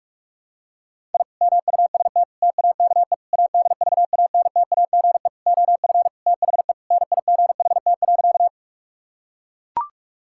Morse code, heard as a keyed tone, about 41 wpm, 700 Hz. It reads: I MUST TAKE ADVANTAGE OF THE NIGHT.
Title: Sentences from Top 1000 Words - Review Part 6 - 45wpm